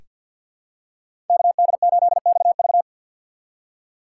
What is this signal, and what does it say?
Morse code, heard as a keyed tone, about 46 wpm, 690 Hz. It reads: KD8XV
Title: USA Call Signs - 2x Character Spacing - 50wpm